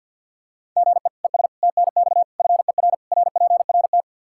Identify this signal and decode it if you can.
Morse code, heard as a keyed tone, about 53 wpm, 700 Hz. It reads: GE ES TNX FER RPRT
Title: Practice QSO - Exchange 1 - Beginning - 50wpm